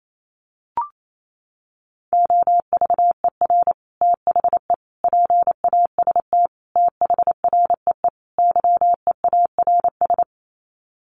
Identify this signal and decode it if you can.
Morse code, heard as a keyed tone, about 28 wpm, 700 Hz. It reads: OVER THE PAST THREE YEARS
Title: Sets of 5 Words Encore2 28wpm